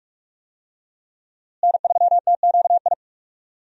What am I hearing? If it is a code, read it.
Morse code, heard as a keyed tone, about 46 wpm, 700 Hz. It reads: N3TQI
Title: USA Call Signs - 5 Characters - Repeated 2x - 45wpm